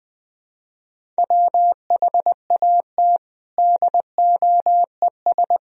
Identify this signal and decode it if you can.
Morse code, heard as a keyed tone, about 19 wpm, 700 Hz. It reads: WHAT DOES
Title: Sentences from Top 1000 Words - Review Part 4 - 20wpm